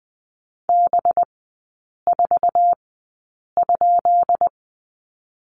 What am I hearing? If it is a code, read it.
Morse code, heard as a keyed tone, about 20 wpm, spaced as for 8 wpm, 690 Hz. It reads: B4?
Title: 212 Lesson 34e - Number 8 - Three Characters 20wpm